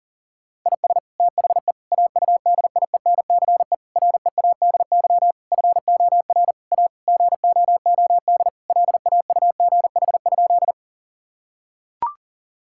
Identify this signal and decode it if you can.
Morse code, heard as a keyed tone, about 40 wpm, 700 Hz. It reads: IS THE AUDIENCE READY FOR A GOOD LAUGH?
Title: Sentences from Top 1000 Words - Review Part 5 - 40wpm